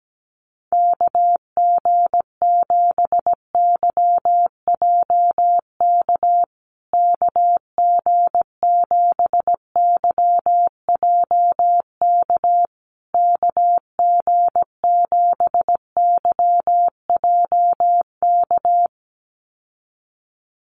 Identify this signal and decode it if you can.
Morse code, heard as a keyed tone, about 17 wpm, 700 Hz. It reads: KG7YJK KG7YJK KG7YJK
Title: USA Call Signs Repeated 3x 17wpm